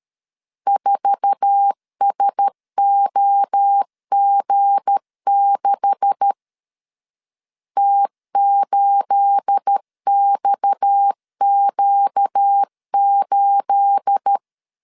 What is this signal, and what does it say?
Morse code, heard as a keyed tone, about 13 wpm, 800 Hz. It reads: 4SOG6 T8XQ8
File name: CW / DET / 1 9 1500C S C 12wpm 3811